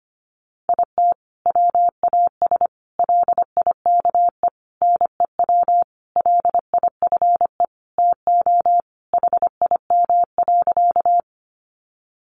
Morse code, heard as a keyed tone, about 25 wpm, 700 Hz. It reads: IT WAS LIKE NEW LIFE TO HIM.
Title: Sentences from Top 300 Words 25wpm